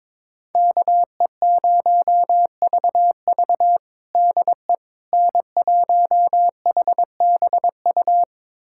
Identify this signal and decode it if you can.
Morse code, heard as a keyed tone, about 21 wpm, 700 Hz. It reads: KE0VV DE N1HBU
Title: USA Callsign DE Callsign - 5 Characters - 22wpm